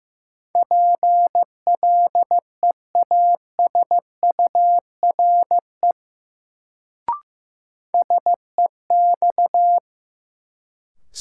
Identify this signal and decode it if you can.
Morse code, heard as a keyed tone, about 15 wpm, 700 Hz. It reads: PLEASURE SEX
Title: Sentences from Top 1600 Words - New Word Drill Alt - 15wpm